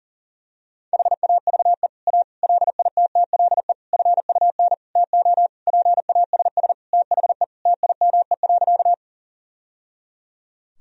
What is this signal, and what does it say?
Morse code, heard as a keyed tone, about 40 wpm, 700 Hz. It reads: HAVE A LITTLE FUN TO PASS THE TIME.
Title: Sentences from Top 1000 Words - Review Part 4 - 40wpm